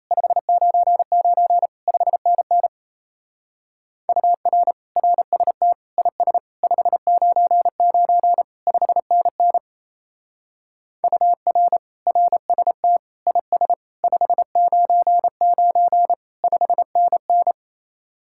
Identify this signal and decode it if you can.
Morse code, heard as a keyed tone, about 35 wpm, 700 Hz. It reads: 599 5NN UR RST IS 599 5NN UR RST IS 599 5NN
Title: Speed-Racing Common QSO Phrases Encore - 28wpm